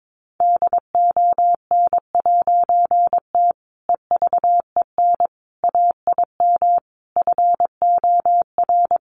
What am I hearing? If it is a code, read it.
Morse code, heard as a keyed tone, about 22 wpm, 700 Hz. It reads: DON'T EVEN AIM FOR
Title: A Sound Of Thunder 22wpm